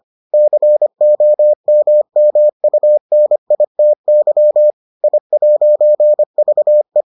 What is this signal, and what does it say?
Morse code, heard as a keyed tone, about 25 wpm, 600 Hz. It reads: COMMUNITY I'VE